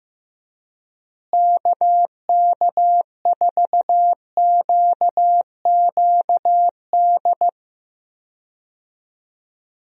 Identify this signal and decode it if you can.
Morse code, heard as a keyed tone, about 15 wpm, 700 Hz. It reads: KK4QQD
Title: USA Call Signs - 6 Characters - Repeated 2x - 15wpm